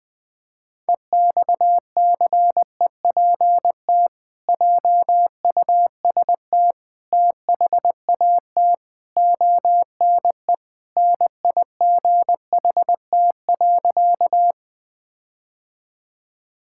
Morse code, heard as a keyed tone, about 20 wpm, 700 Hz. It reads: EXCEPT JUST THAT ONE NIGHT.
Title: Sentences from Top 1600 Words 20wpm